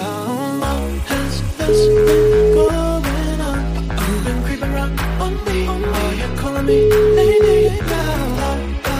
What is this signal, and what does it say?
Ringback tone (Europe, 425 Hz), with noise.